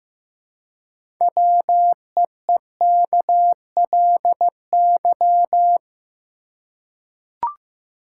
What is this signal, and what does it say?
Morse code, heard as a keyed tone, about 15 wpm, 700 Hz. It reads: WEEKLY
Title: Six Letter Words - 15wpm